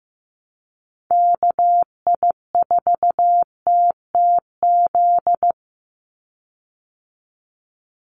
Morse code, heard as a keyed tone, about 15 wpm, 700 Hz. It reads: KI4TTZ
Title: USA Call Signs 15wpm